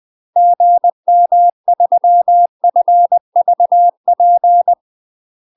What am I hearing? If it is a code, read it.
Morse code, heard as a keyed tone, about 20 wpm, 700 Hz. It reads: GM3FVP
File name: UK MASTER QSOs TRAINING / 20WPM / UK MASTER QSOs 0060